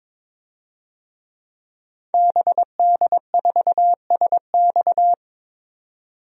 Morse code, heard as a keyed tone, about 22 wpm, 700 Hz. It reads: BD4SX
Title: DXCC-100 Call Signs - 5 Characters - 22wpm